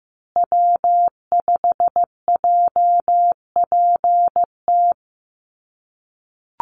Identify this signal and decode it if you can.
Morse code, heard as a keyed tone, about 15 wpm, 700 Hz. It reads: W5JPT
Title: USA Call Signs Repeated 2x 15wpm